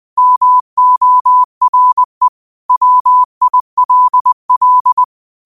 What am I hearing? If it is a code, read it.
Morse code, heard as a keyed tone, about 20 wpm, 1000 Hz. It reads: MORE WILL
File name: qotdfiles / 20wpm